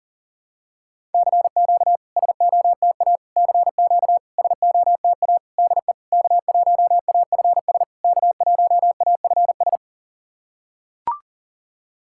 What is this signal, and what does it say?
Morse code, heard as a keyed tone, about 40 wpm, 690 Hz. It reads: CQ SOTA CQ SOTA DE K1AFS K1AFS
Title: SOTA QSO Practice 40wpm